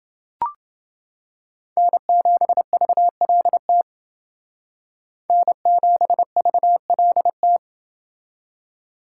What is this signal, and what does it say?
Morse code, heard as a keyed tone, about 28 wpm, 700 Hz. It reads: N7VLT N7VLT